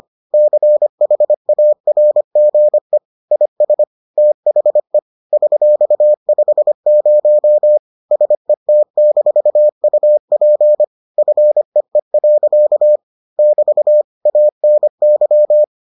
Morse code, heard as a keyed tone, about 25 wpm, 600 Hz. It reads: CHARGE IS THE $50 SET-UP FEE. <BT> ANY